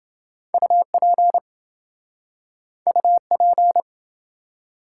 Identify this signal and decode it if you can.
Morse code, heard as a keyed tone, about 27 wpm, 700 Hz. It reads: UP UP